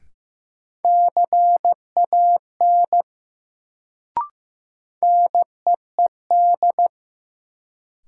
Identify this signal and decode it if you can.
Morse code, heard as a keyed tone, about 15 wpm, 700 Hz. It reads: CAN NEED
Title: Sets of 2 Words - New Word Drill - 15wpm